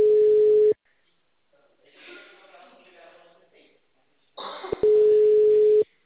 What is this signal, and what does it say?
Ringback tone (Europe, 425 Hz), with speech.
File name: out- -2039- - - .97298